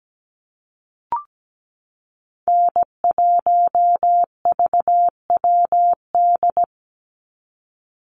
Morse code, heard as a keyed tone, about 17 wpm, 700 Hz. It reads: N1VWD
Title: NA SOTA Chaser Callsigns 17wpm